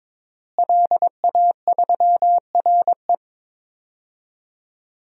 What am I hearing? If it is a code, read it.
Morse code, heard as a keyed tone, about 22 wpm, 700 Hz. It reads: LA3RE
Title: DXCC-100 Call Signs - 5 Characters - Repeated 2x - 22wpm